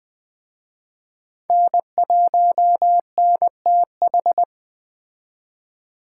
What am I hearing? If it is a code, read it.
Morse code, heard as a keyed tone, about 21 wpm, 700 Hz. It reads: N1NTH
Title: Speed Racing - USA 1x3 Call Signs 17wpm